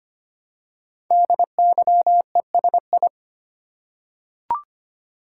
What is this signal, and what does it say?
Morse code, heard as a keyed tone, about 25 wpm, 700 Hz. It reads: DYESI
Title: Code Groups - 5 Letters - 25wpm